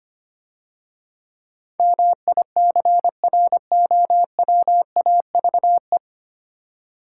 Morse code, heard as a keyed tone, about 27 wpm, 700 Hz. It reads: MICROWAVE
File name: Easy Words - Speed-Racing with Hints - 25wpm